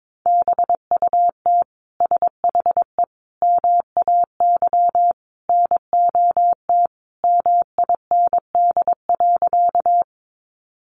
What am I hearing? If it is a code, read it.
Morse code, heard as a keyed tone, about 22 wpm, 700 Hz. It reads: BUT SHE MAY NOT MIND.
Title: Sentences from Top 600 Words 22wpm